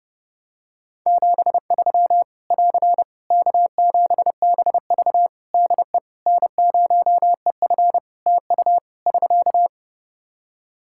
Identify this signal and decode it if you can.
Morse code, heard as a keyed tone, about 30 wpm, 700 Hz. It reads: 73 <AR> K7BV DE N0EF TU <SK>
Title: Practice QSO - Exchange 4 - Ending - 30wpm